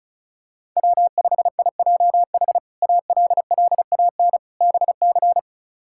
Morse code, heard as a keyed tone, about 34 wpm, 700 Hz. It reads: W5IJH ALLAN BC
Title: K1USN SST QSO Practice - 35wpm